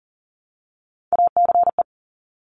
Morse code, heard as a keyed tone, about 38 wpm, 700 Hz. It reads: ACE
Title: Top Three Letter Combinations in Words 40wpm